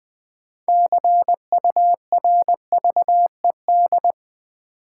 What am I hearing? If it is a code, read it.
Morse code, heard as a keyed tone, about 20 wpm, 690 Hz. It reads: CURVED
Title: Build A Word - 2 Letters At A Time - 6-Letter Words - 20wpm